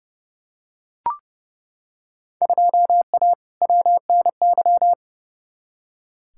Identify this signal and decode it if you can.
Morse code, heard as a keyed tone, about 30 wpm, 700 Hz. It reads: 2A WNY
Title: Field Day Exchanges - 30wpm